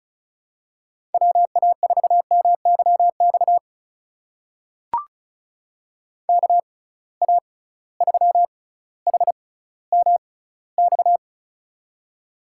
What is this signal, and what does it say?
Morse code, heard as a keyed tone, about 34 wpm, spaced as for 18 wpm, 700 Hz. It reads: WA4MYX KA3HMX
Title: USA Call Signs - 6 Characters - 6x Character Spacing - 35wpm